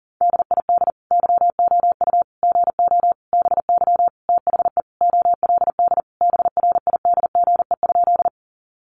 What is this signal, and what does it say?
Morse code, heard as a keyed tone, about 40 wpm, 700 Hz. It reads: DID YOU GO BY THE OLD BRIDGE?
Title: Sentences from Top 1200 Words 40wpm